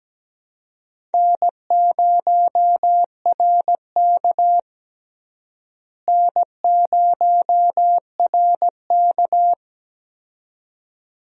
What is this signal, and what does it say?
Morse code, heard as a keyed tone, about 17 wpm, 700 Hz. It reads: N0RK N0RK
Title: USA Call Signs - 4 Characters - Repeated 2x - 17wpm